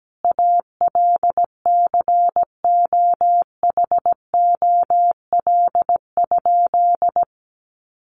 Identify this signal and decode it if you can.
Morse code, heard as a keyed tone, about 17 wpm, 700 Hz. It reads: ALCOHOL?
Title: Sentences from Top 1400 Words 17wpm